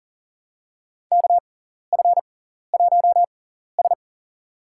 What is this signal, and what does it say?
Morse code, heard as a keyed tone, about 38 wpm, spaced as for 14 wpm, 700 Hz. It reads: KF1S
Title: USA Call Signs - 4 Characters - 6x Character Spacing - 40wpm